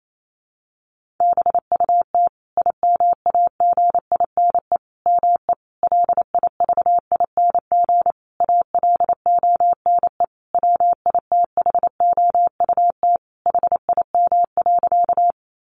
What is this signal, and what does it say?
Morse code, heard as a keyed tone, about 28 wpm, 700 Hz. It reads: BUT IMAGINE ME LIVING ALONE WITHOUT HIM.
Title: Sentences from Top 1400 Words 28wpm